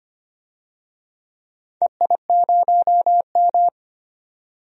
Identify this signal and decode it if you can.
Morse code, heard as a keyed tone, about 25 wpm, 700 Hz. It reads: EI0M
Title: DXCC-100 Call Signs - 4 Characters - 25wpm